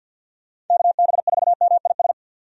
Morse code, heard as a keyed tone, about 49 wpm, 700 Hz. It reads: KB4MIS
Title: USA Call Signs - 6 Characters - 50wpm